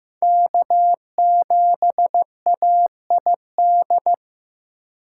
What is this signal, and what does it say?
Morse code, heard as a keyed tone, about 18 wpm, 700 Hz. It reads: K7AID
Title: Speed Racing - USA 1x3 Call Signs 15wpm